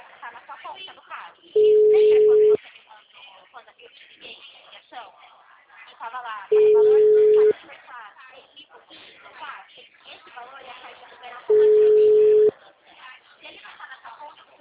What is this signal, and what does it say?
Ringback tone (Europe, 425 Hz), with speech.